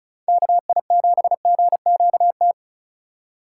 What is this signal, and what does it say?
Morse code, heard as a keyed tone, about 36 wpm, 700 Hz. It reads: KI7GQT
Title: USA Call Signs - 6 Characters - 4x Character Spacing - 35wpm